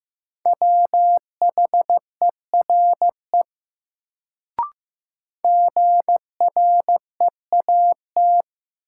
Morse code, heard as a keyed tone, about 15 wpm, 700 Hz. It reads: WHERE GREAT
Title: Sentences from Top 200 Words - New Word Drill - 15wpm